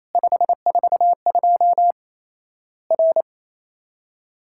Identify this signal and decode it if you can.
Morse code, heard as a keyed tone, about 28 wpm, 700 Hz. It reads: 542 R
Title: Worked All Europe DX Contest - QTC Exchange - 28wpm